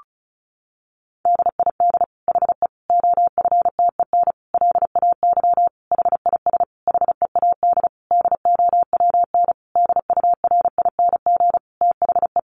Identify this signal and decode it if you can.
Morse code, heard as a keyed tone, about 34 wpm, 700 Hz. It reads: DID HE OFTEN LAY HIS HEAD DOWN DURING THE
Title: Sentences from Top 1000 Words 35wpm